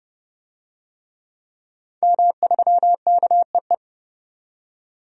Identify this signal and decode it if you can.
Morse code, heard as a keyed tone, about 30 wpm, 700 Hz. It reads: M3KEE
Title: DXCC-100 Call Signs - 5 Characters - 30wpm